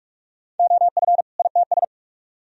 Morse code, heard as a keyed tone, about 44 wpm, 700 Hz. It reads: OF ITS